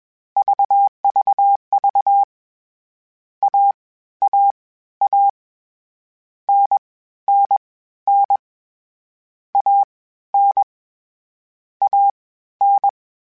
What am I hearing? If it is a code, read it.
Morse code, heard as a keyed tone, about 21 wpm, spaced as for 13 wpm, 800 Hz. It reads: VVV AAA NNN AN AN